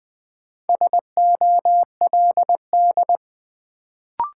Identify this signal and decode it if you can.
Morse code, heard as a keyed tone, about 20 wpm, 700 Hz. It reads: SOLD
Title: Sentences from Top 1400 Words - New Word Drill - 20wpm